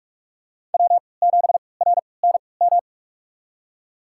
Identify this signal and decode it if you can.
Morse code, heard as a keyed tone, about 46 wpm, spaced as for 26 wpm, 700 Hz. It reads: W7RNM
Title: USA Call Signs - 3x Character Spacing - 45wpm